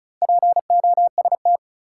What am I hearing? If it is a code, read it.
Morse code, heard as a keyed tone, about 36 wpm, 700 Hz. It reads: POST
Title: Sentences from Top 1000 Words - New Word Drill - 35wpm